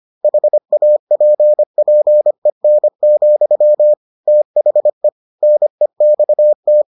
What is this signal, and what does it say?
Morse code, heard as a keyed tone, about 25 wpm, 600 Hz. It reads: HAPPEN, THE NEXT